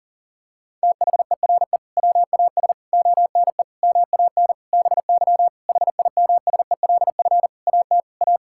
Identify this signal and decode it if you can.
Morse code, heard as a keyed tone, about 40 wpm, 700 Hz. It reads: THERE WAS ONE MAN BY HIMSELF AT A
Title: Sentences from Top 1600 Words 40wpm